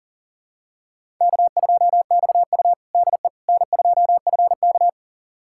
Morse code, heard as a keyed tone, about 37 wpm, 700 Hz. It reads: K2XU DE N2FK
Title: USA Callsign DE Callsign - 4 Characters - 40wpm